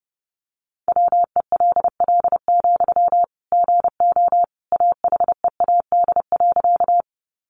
Morse code, heard as a keyed tone, about 30 wpm, 700 Hz. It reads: WELL, GO AHEAD.
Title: Sentences from Top 1100 Words 30wpm